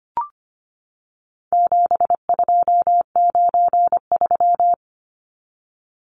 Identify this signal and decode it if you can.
Morse code, heard as a keyed tone, about 25 wpm, 700 Hz. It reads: 7293
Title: Numbers - 4-Digit 25wpm